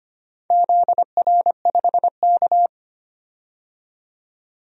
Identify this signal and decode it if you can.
Morse code, heard as a keyed tone, about 25 wpm, 700 Hz. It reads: ZR5K
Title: DXCC-100 Call Signs - 4 Characters - 25wpm